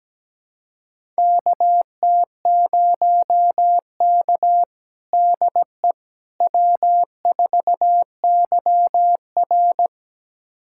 Morse code, heard as a keyed tone, about 17 wpm, 700 Hz. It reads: KT0K DE W4YR
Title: USA Callsign DE Callsign - 4 Characters - 17wpm